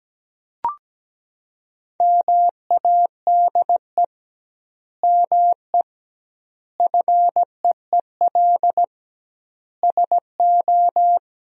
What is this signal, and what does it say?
Morse code, heard as a keyed tone, about 17 wpm, 700 Hz. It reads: MADE ME FEEL SO
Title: 2x Word-Spacing - Sentences from Top 400 Words - 17wpm